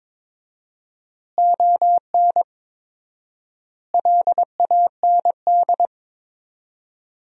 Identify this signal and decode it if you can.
Morse code, heard as a keyed tone, about 22 wpm, 700 Hz. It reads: ON LAND
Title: 4x Word-Spacing - Sentences from Top 500 Words - 22wpm